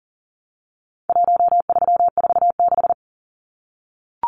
Morse code, heard as a keyed tone, about 40 wpm, 700 Hz. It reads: 1346
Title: Numbers - Contest Style - 4-Digit 40wpm